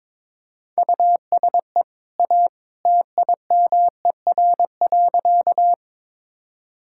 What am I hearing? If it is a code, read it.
Morse code, heard as a keyed tone, about 22 wpm, 690 Hz. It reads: USE A TIMER.
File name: practice / 22wpm / Easy Sentences with Hints - 22wpm